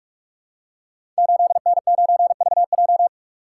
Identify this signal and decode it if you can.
Morse code, heard as a keyed tone, about 44 wpm, 690 Hz. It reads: 8N9VJ
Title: Code Groups - 5 Letter:Numbers - 45wpm